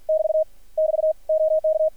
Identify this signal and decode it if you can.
Morse code, heard as a keyed tone, about 53 wpm, 590 Hz. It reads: <BT> <BT> OK
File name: archived / CRSnet